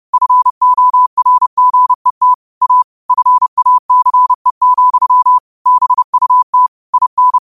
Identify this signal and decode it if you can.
Morse code, heard as a keyed tone, about 30 wpm, 1000 Hz. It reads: FORGET A FACE, BUT IN